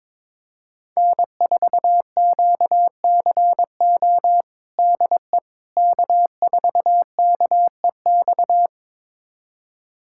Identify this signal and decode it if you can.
Morse code, heard as a keyed tone, about 22 wpm, 700 Hz. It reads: N4QCO DE K4KEX
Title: USA Callsign DE Callsign 22wpm